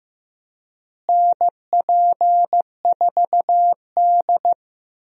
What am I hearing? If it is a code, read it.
Morse code, heard as a keyed tone, about 15 wpm, 700 Hz. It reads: NP4D
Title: USA Call Signs - 4 Characters - Repeated 3x - 15wpm